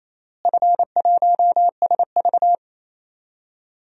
Morse code, heard as a keyed tone, about 28 wpm, 700 Hz. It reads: F1SV
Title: DXCC-100 Call Signs - 4 Characters - Repeated 3x - 28wpm